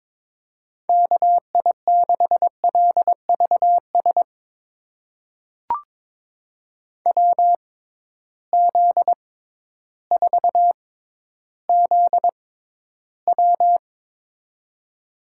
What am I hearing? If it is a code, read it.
Morse code, heard as a keyed tone, about 22 wpm, spaced as for 12 wpm, 700 Hz. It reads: KI6LVS WZ4ZW
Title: USA Call Signs - 6x Character Spacing - 22wpm